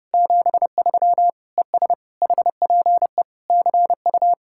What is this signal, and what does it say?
Morse code, heard as a keyed tone, about 36 wpm, 700 Hz. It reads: 73 ES HPE CU
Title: Speed-Racing Common QSO Phrases - 30wpm